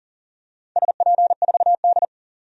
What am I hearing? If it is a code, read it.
Morse code, heard as a keyed tone, about 40 wpm, 700 Hz. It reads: SP4D
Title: DXCC-100 Call Signs - 4 Characters - 40wpm